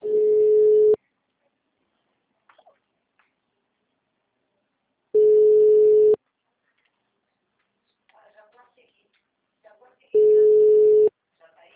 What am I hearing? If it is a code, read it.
Ringback tone (Europe, 425 Hz), with speech.